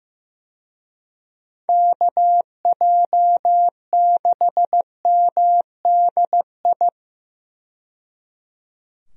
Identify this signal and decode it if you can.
Morse code, heard as a keyed tone, about 15 wpm, 700 Hz. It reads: KJ6MDI
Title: USA Call Signs - 6 Characters - 15wpm